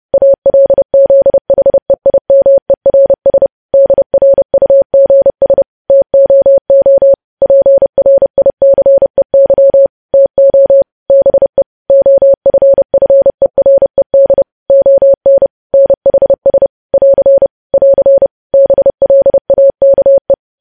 Morse code, heard as a keyed tone, about 30 wpm, 550 Hz. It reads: ALZHEIMERS DRUGS TOO PRICEY TO BE OFFERED ON NHS <AR> <AR> BLAKE
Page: news headlines in morse code at 30 wpm